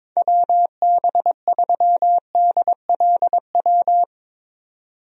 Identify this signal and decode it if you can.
Morse code, heard as a keyed tone, about 22 wpm, 700 Hz. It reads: WB3DLW
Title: USA Call Signs - 6 Characters - 22wpm